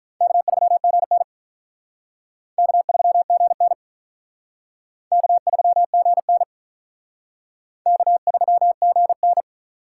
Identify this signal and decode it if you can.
Morse code, heard as a keyed tone, about 42 wpm, 700 Hz. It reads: K3GN K3GN K3GN K3GN
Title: Speed Racing - USA 1x2 Call Signs 35wpm